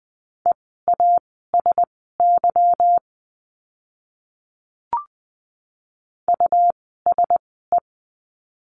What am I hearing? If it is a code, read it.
Morse code, heard as a keyed tone, about 20 wpm, spaced as for 14 wpm, 700 Hz. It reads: EASY USE
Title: 146 Lesson 24f - Letter y - Words 20wpm